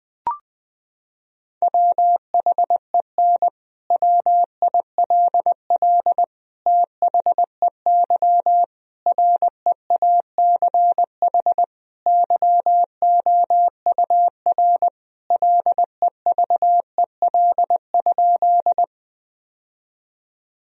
Morse code, heard as a keyed tone, about 20 wpm, 700 Hz. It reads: WHEN WILL THEY REACH YOUR LEVEL?
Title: Sentences from Top 1000 Words - Review Part 4 - 20wpm